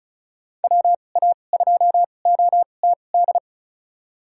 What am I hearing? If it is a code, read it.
Morse code, heard as a keyed tone, about 36 wpm, spaced as for 25 wpm, 700 Hz. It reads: WA2OTD
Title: USA Call Signs - 2x Character Spacing - 35wpm